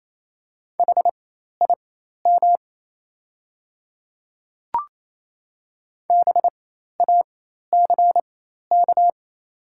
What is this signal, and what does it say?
Morse code, heard as a keyed tone, about 28 wpm, spaced as for 13 wpm, 700 Hz. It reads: HIM BACK
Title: 4x Character Spacing - Top 300 Words - 28wpm